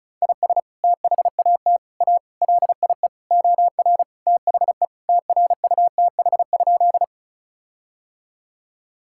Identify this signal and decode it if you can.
Morse code, heard as a keyed tone, about 34 wpm, 700 Hz. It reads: IS THAT A LIE OR THE TRUTH?
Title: Sentences from Top 1600 Words 35wpm